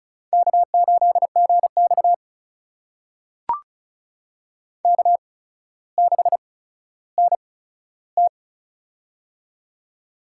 Morse code, heard as a keyed tone, about 36 wpm, spaced as for 15 wpm, 700 Hz. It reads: K8GX K6NT
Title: USA Call Signs - 4 Characters - 8x Character Spacing - 35wpm